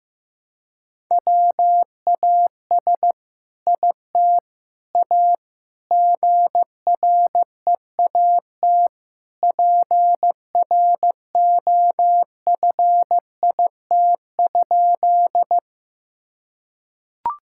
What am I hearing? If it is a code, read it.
Morse code, heard as a keyed tone, about 15 wpm, 700 Hz. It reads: WAS IT A GREAT PROFIT?
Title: Sentences from Top 1000 Words - Review Part 5 - 15wpm